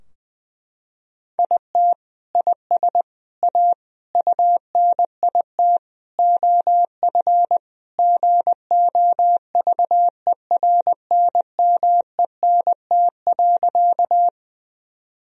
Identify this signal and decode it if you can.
Morse code, heard as a keyed tone, about 20 wpm, 700 Hz. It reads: IT IS A UNIT OF GOVERNMENT.
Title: Sentences from Top 1000 Words - Review Part 2 - 20wpm